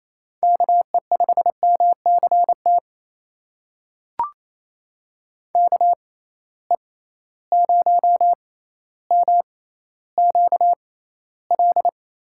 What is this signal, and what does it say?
Morse code, heard as a keyed tone, about 28 wpm, spaced as for 14 wpm, 700 Hz. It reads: KE5MCT KE0MQL
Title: USA Call Signs - 6 Characters - 6x Character Spacing - 28wpm